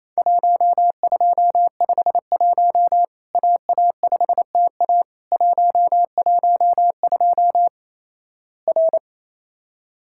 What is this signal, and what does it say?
Morse code, heard as a keyed tone, about 28 wpm, 700 Hz. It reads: 1251 AA5TA 112 R
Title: Worked All Europe DX Contest - QTC Exchange - 28wpm